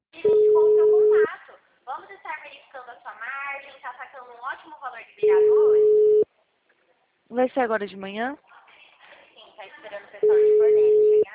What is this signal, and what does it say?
Ringback tone (Europe, 425 Hz), with speech.